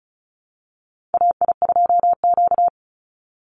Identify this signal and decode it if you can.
Morse code, heard as a keyed tone, about 34 wpm, 700 Hz. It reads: AI2Q
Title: Call Signs Repeated 3x - 35wpm